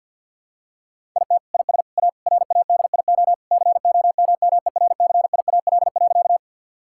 Morse code, heard as a keyed tone, about 49 wpm, 700 Hz. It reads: IT IS A RADIO COMMERCIAL.
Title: Sentences from Top 1000 Words - Review Part 5 - 50wpm